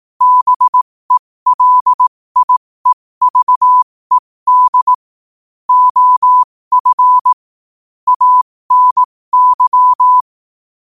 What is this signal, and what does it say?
Morse code, heard as a keyed tone, about 18 wpm, spaced as for 15 wpm, 1000 Hz. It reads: BELIEVED OF ANY
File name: qotdfiles / 15wpm